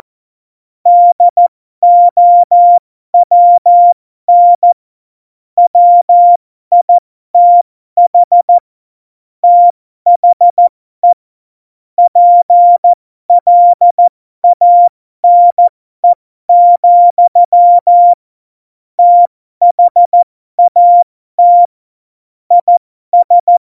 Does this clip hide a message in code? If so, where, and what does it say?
Yes: Morse code from 0.9 to 23.6 s, heard as a keyed tone, about 14 wpm, spaced as for 12 wpm, 700 Hz. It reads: DOWN WITH THE PLANE, THAT IS